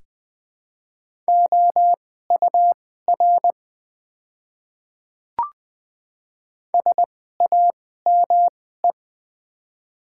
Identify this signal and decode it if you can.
Morse code, heard as a keyed tone, about 20 wpm, spaced as for 15 wpm, 700 Hz. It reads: OUR SAME
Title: 2x Character Spacing - Top 200 Words - 20wpm